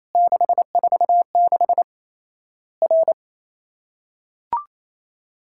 Morse code, heard as a keyed tone, about 28 wpm, 700 Hz. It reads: 646 R
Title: Worked All Europe DX Contest - QTC Exchange - 28wpm